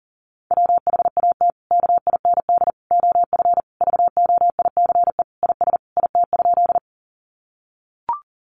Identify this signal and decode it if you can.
Morse code, heard as a keyed tone, about 37 wpm, 700 Hz. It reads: WHAT KIND OF VOICE IS IT?
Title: Sentences from Top 500 Words 40wpm